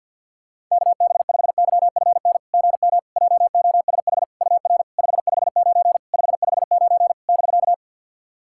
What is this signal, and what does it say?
Morse code, heard as a keyed tone, about 50 wpm, 700 Hz. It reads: KB5YFN GM JOSH UR 559 559 <BK>
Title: SOTA QSO Practice 50wpm